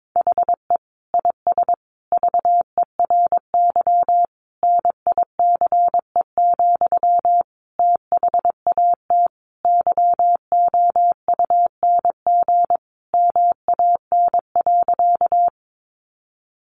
Morse code, heard as a keyed tone, about 22 wpm, 700 Hz. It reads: HE IS VERY NICE, THAT YOUNG MAN.
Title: Sentences from Top 1100 Words 22wpm